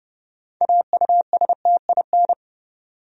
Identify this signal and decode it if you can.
Morse code, heard as a keyed tone, about 31 wpm, 700 Hz. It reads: AUSTIN
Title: USA Common Names - 30wpm